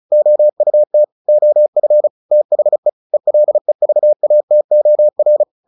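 Morse code, heard as a keyed tone, about 34 wpm, 600 Hz. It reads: OUT OF THE ELEVATOR
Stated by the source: Morse practice at 35 wpm from CW med Gnister, sending Ken Follett's novel Never in English.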